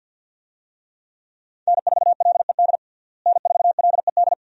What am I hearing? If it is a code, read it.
Morse code, heard as a keyed tone, about 50 wpm, 700 Hz. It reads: N4LED N4LED
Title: USA Call Signs - 5 Characters - Repeated 2x - 50wpm